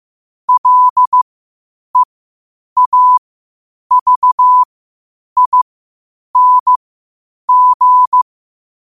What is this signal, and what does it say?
Morse code, heard as a keyed tone, about 15 wpm, spaced as for 8 wpm, 1000 Hz. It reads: LEAVING
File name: qotdfiles / 8wpm